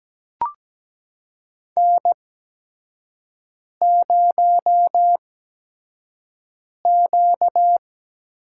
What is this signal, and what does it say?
Morse code, heard as a keyed tone, about 17 wpm, spaced as for 5 wpm, 700 Hz. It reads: N0Q
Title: USA Call Signs - 4 Characters - 8x Character Spacing - 17wpm